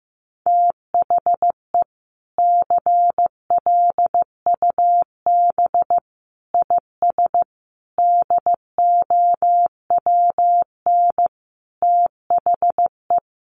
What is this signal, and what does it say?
Morse code, heard as a keyed tone, about 15 wpm, 700 Hz. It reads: THE CLUB IS DOWN THE
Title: Sentences from Top 1200 Words 15wpm